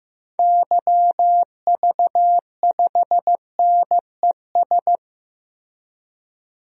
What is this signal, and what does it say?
Morse code, heard as a keyed tone, about 15 wpm, 700 Hz. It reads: YV5NES
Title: DXCC-100 Call Signs - 6 Characters - Repeated 2x - 15wpm